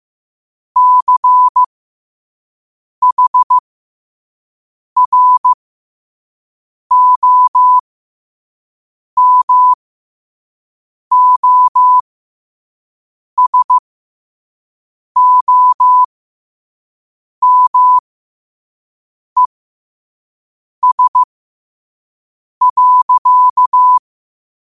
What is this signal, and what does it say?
Morse code, heard as a keyed tone, about 15 wpm, spaced as for 5 wpm, 1000 Hz. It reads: CHROMOSOMES.
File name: qotdfiles / 5wpm